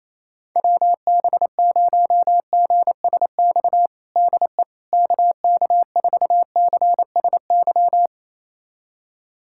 Morse code, heard as a keyed tone, about 27 wpm, 700 Hz. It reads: WB0GSX DE KK4CSY
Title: USA Callsign DE Callsign - 6 Characters - 28wpm